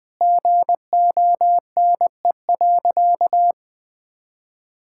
Morse code, heard as a keyed tone, about 20 wpm, 700 Hz. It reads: GONE.
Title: Sentences with 3-Word Phrases - 20wpm